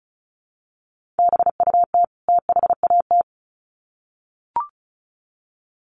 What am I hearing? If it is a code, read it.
Morse code, heard as a keyed tone, about 34 wpm, 700 Hz. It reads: BUT THAT
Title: Sets of 2 Words 35wpm